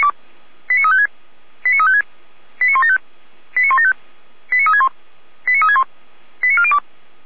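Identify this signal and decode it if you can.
Five-tone selective call (ZVEI), heard as quick runs of five tones, tones of about 70 ms. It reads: 89356 89356 89175 89175 89361 89361 89402
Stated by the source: ZVEI five-tone call received by SDR.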